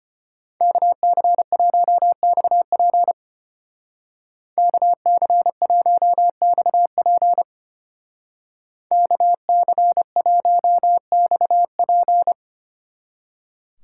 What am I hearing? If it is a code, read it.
Morse code, heard as a keyed tone, about 30 wpm, 700 Hz. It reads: KC1XP KC1XP KC1XP
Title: Speed Racing - USA 2x2 Call Signs 25wpm